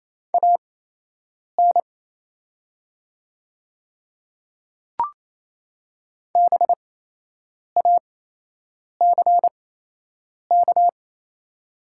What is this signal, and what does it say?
Morse code, heard as a keyed tone, about 27 wpm, spaced as for 8 wpm, 700 Hz. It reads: AN BACK
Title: 8x Character Spacing - Top 200 Words - 28wpm